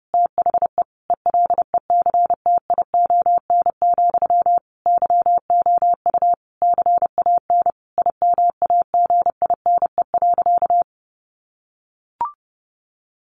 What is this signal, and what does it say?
Morse code, heard as a keyed tone, about 28 wpm, 700 Hz. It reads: THE ELECTION, YOU CAN IMAGINE.
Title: Sentences from Top 1500 Words 30wpm